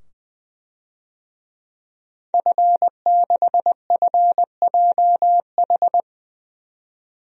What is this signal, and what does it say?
Morse code, heard as a keyed tone, about 20 wpm, 700 Hz. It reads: F6FJH
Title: DXCC-100 Call Signs - 5 Characters - 20wpm